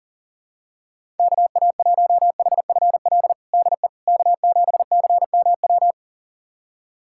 Morse code, heard as a keyed tone, about 40 wpm, 700 Hz. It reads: KA1HFL DE K7CMW
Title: USA Callsign DE Callsign 40wpm